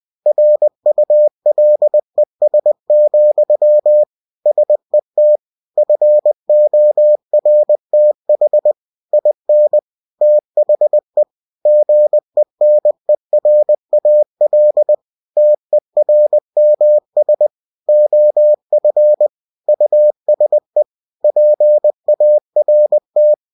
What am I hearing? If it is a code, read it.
Morse code, heard as a keyed tone, about 20 wpm, 600 Hz. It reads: RULES, SET FORTH IN THE GENERAL TERMS OF USE PART
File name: cw-books / winnie-the-pooh / winnie-20wpm-page-0112